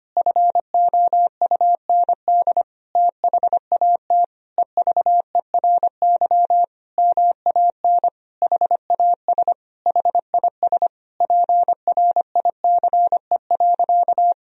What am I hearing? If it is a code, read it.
Morse code, heard as a keyed tone, about 25 wpm, 700 Hz. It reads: FOUND THAT EVERY MAN HAS HIS PRICE.
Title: Sentences from Top 1000 Words - Review Part 4 - 25wpm